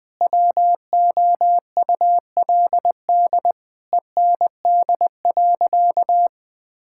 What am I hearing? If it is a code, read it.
Morse code, heard as a keyed tone, about 20 wpm, 700 Hz. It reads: WOULD END.
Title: Sentences from Top 1600 Words 20wpm